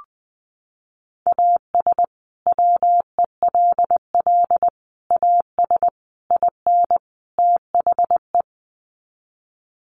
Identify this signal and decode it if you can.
Morse code, heard as a keyed tone, about 20 wpm, 700 Hz. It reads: AS WELL AS IN THE
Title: Sets of 5 Words Encore 20wpm